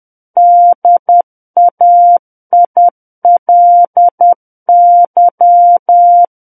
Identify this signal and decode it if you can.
Morse code, heard as a keyed tone, about 10 wpm, 700 Hz. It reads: DAILY
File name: Book One / 01-Chapter 1 - 10 WPM